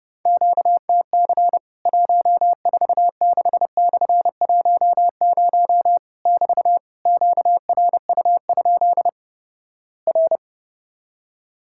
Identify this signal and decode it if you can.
Morse code, heard as a keyed tone, about 32 wpm, 700 Hz. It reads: QTC 146/10 <BT> QRU? R
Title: Worked All Europe DX Contest - QTC Exchange - 30wpm